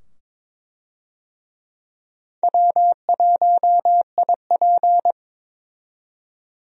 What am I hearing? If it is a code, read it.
Morse code, heard as a keyed tone, about 22 wpm, 700 Hz. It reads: W1IP
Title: USA Call Signs - 4 Characters - 22wpm